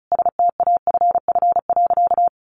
Morse code, heard as a keyed tone, about 34 wpm, 700 Hz. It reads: STAFF.
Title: Sentences from Top 1200 Words 35wpm